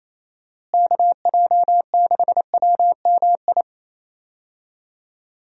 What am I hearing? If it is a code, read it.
Morse code, heard as a keyed tone, about 28 wpm, 700 Hz. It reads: KJ6WMI
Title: USA Call Signs - 6 Characters - Repeated 3x - 28wpm